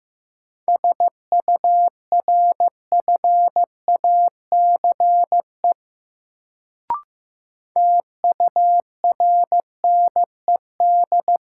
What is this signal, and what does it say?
Morse code, heard as a keyed tone, about 15 wpm, 700 Hz. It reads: SURFACE TURNED
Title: Sets of 4 Words - New Word Drill - 15wpm